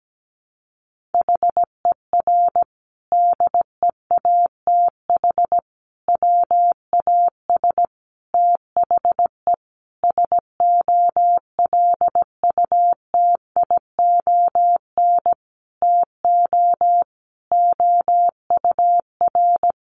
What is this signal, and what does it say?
Morse code, heard as a keyed tone, about 17 wpm, 700 Hz. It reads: HER DEATH WAS THE SOLUTION TO OUR
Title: Sentences from Top 1300 Words 17wpm